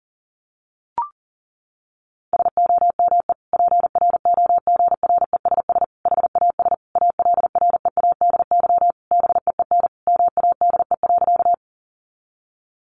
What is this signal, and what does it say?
Morse code, heard as a keyed tone, about 40 wpm, 700 Hz. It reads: SOME PROGRESS HAS ALREADY BEEN MADE.
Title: Sentences from Top 1400 Words 40wpm